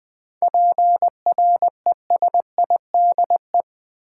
Morse code, heard as a keyed tone, about 20 wpm, 700 Hz. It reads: PRESIDE
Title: 7 Letter Words - First 3 Letters Given - 20wpm